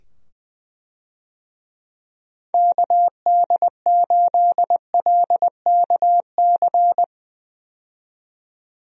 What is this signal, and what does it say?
Morse code, heard as a keyed tone, about 20 wpm, 700 Hz. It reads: KD8LKC
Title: USA Call Signs - 6 Characters - 20wpm